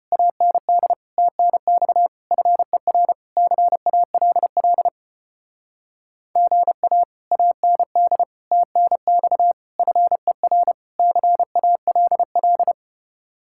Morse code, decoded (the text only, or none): AND TNX FER CALL GA AND TNX FER CALL